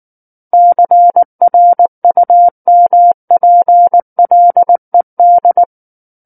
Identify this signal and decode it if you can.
Morse code, heard as a keyed tone, about 19 wpm, 700 Hz. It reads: CRUMPLED
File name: Book One / 10-Chapter 10 - 19 WPM